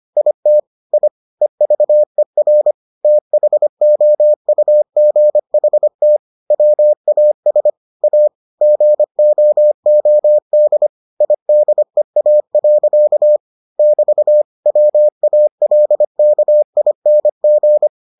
Morse code, heard as a keyed tone, about 25 wpm, 600 Hz. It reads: IT I EVER THOUGHT WAS A GOOD IDEA. <BT> WALKING